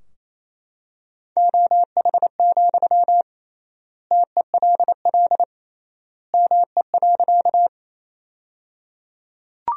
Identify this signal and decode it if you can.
Morse code, heard as a keyed tone, about 28 wpm, 700 Hz. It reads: OH, TELL ME.
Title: 3x Word-Spacing - Sentences from Top 500 Words - 28wpm